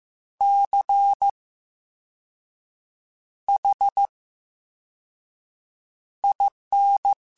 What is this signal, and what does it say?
Morse code, heard as a keyed tone, about 15 wpm, 800 Hz. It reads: C H I N
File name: Special INGs-mp3s / Special INGs-mp3s / VerySlowToFaster Ing @13